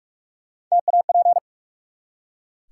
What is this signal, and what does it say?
Morse code, heard as a keyed tone, about 44 wpm, 700 Hz. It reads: TAP
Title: Three Letter Words - 45wpm